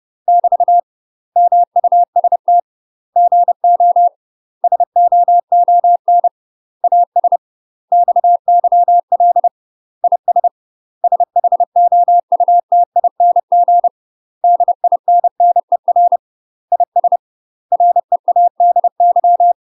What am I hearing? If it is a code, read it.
Morse code, heard as a keyed tone, about 30 wpm, 700 Hz. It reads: <BT> MUST GO SOON AS XYL IS SHOUTING DINNER IS READY
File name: UK MASTER QSOs TRAINING / 30WPM / UK MASTER QSOs 0029